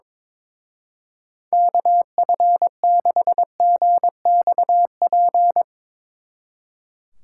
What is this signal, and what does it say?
Morse code, heard as a keyed tone, about 24 wpm, 700 Hz. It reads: KF6GXP
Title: Speed Racing - USA Call Signs 22wpm